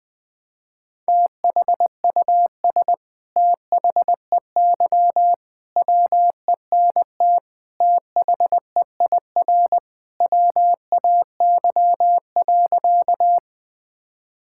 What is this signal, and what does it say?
Morse code, heard as a keyed tone, about 20 wpm, 700 Hz. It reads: THUS THEY WENT THEIR WAY.
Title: Sentences from Top 1000 Words - Review Part 5 - 20wpm